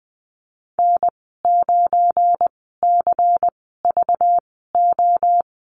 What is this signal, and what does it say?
Morse code, heard as a keyed tone, about 20 wpm, spaced as for 15 wpm, 700 Hz. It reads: N9CVO
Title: 238 Lesson 38g - Letter q - Call Signs 20wpm